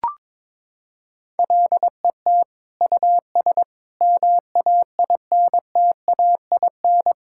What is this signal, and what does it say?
Morse code, heard as a keyed tone, about 22 wpm, 700 Hz. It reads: LET US MAINTAIN
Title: Sentences from Top 1000 Words - Review Part 5 - 22wpm